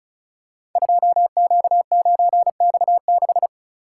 Morse code, heard as a keyed tone, about 36 wpm, 690 Hz. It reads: 2Q9X6
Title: Code Groups - 5 Letter:Numbers - 35wpm